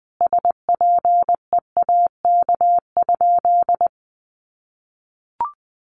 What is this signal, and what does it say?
Morse code, heard as a keyed tone, about 20 wpm, 700 Hz. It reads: SPEAK?
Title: Sentences from Top 1100 Words 20wpm